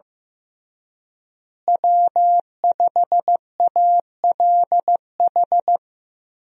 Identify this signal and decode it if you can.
Morse code, heard as a keyed tone, about 15 wpm, 700 Hz. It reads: W5ALH
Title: USA Call Signs - 5 Characters - Repeated 3x - 15wpm